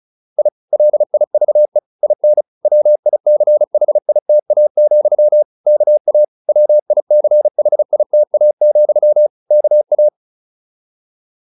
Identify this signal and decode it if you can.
Morse code, heard as a keyed tone, about 34 wpm, 600 Hz. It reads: I LIVE IN WICHITA, KA WICHITA, KA